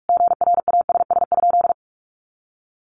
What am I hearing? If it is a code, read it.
Morse code, heard as a keyed tone, about 44 wpm, 700 Hz. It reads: GRASS?
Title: A Sound Of Thunder 45wpm